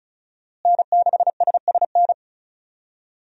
Morse code, heard as a keyed tone, about 35 wpm, 700 Hz. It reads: N6SSN
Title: USA Call Signs - 5 Characters - Repeated 3x - 35wpm